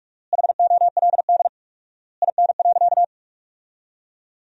Morse code, heard as a keyed tone, about 43 wpm, 700 Hz. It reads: HOLD IN.
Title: 4x Word-Spacing - Sentences from Top 400 Words - 45wpm